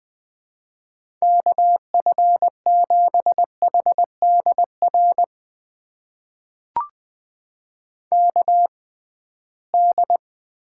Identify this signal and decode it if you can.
Morse code, heard as a keyed tone, about 21 wpm, spaced as for 15 wpm, 700 Hz. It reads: KF7HDR KD
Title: USA Call Signs - 6 Characters - 6x Character Spacing - 20wpm